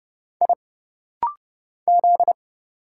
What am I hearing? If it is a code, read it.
Morse code, heard as a keyed tone, about 30 wpm, 700 Hz. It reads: I Z
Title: Single Letters - Rapid-Fire 30wpm